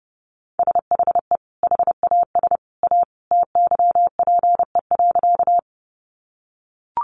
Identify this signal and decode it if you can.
Morse code, heard as a keyed tone, about 30 wpm, 700 Hz. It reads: SHE HAS A TYPE.
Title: Sentences from Top 600 Words 30wpm